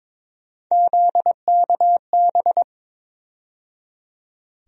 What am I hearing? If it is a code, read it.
Morse code, heard as a keyed tone, about 22 wpm, 700 Hz. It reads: ZKB
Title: USA Call Sign Postfixes - 3 Letters 22wpm